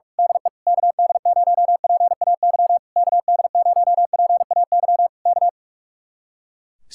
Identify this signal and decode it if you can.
Morse code, heard as a keyed tone, about 44 wpm, 700 Hz. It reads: DE KD0PAY KD0PAY K
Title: Speed-Racing Common QSO Phrases Encore - 45wpm